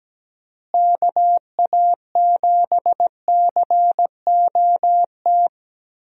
Morse code, heard as a keyed tone, about 17 wpm, 700 Hz. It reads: KA7COT
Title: USA Call Signs - 6 Characters - Repeated 2x - 17wpm